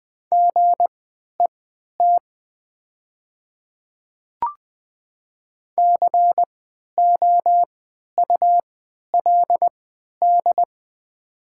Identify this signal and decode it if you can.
Morse code, heard as a keyed tone, about 20 wpm, spaced as for 11 wpm, 700 Hz. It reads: GET COULD
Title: 3x Character Spacing - Top 500 Words - 20wpm